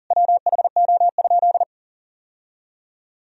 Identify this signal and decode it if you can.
Morse code, heard as a keyed tone, about 40 wpm, 700 Hz. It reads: WHO?
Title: Sentences from Top 1000 Words - Review Part 4 - 40wpm